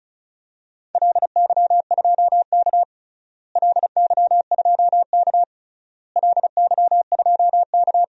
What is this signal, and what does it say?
Morse code, heard as a keyed tone, about 36 wpm, 700 Hz. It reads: LY2K LY2K LY2K
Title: DXCC-100 Call Signs - 4 Characters - Repeated 3x - 35wpm